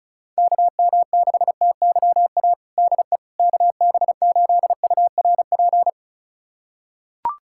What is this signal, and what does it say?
Morse code, heard as a keyed tone, about 34 wpm, 700 Hz. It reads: KM6TYA DE KB8URP
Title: USA Callsign DE Callsign 35wpm